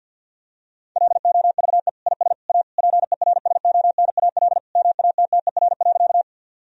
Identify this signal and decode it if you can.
Morse code, heard as a keyed tone, about 50 wpm, 700 Hz. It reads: LOVE IS A PERSONAL MATTER.
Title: Sentences from Top 1000 Words - Review Part 4 - 50wpm